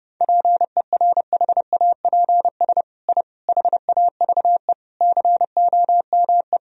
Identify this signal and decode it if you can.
Morse code, heard as a keyed tone, about 28 wpm, 700 Hz. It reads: PERHAPS I HAVE COME
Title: Sentences from Top 1000 Words - Review Part 2 - 30wpm